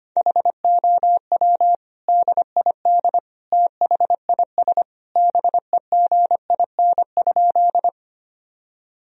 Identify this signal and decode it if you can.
Morse code, heard as a keyed tone, about 25 wpm, 700 Hz. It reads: HOW DID THIS BEGIN?
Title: Sentences from Top 1000 Words - Review Part 1 - 25wpm